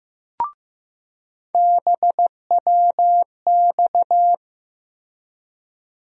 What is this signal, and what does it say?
Morse code, heard as a keyed tone, about 15 wpm, 700 Hz. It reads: BWX
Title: USA Call Sign Postfixes - 3 Letters 15wpm